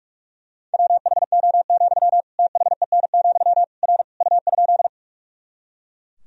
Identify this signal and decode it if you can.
Morse code, heard as a keyed tone, about 42 wpm, 700 Hz. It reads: WHO, THEN, R U?
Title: Sentences from Top 100 Words and Abbreviations 45wpm